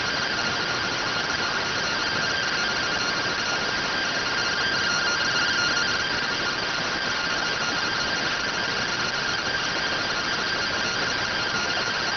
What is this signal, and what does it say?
Two-tone teleprinter signal, about 170 Hz shift.